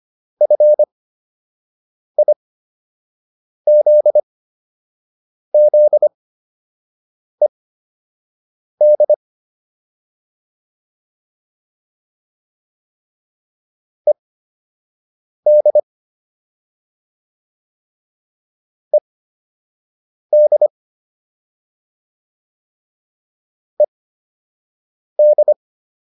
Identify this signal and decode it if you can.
Morse code, heard as a keyed tone, about 25 wpm, spaced as for 6 wpm, 600 Hz. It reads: FIZZED ED ED ED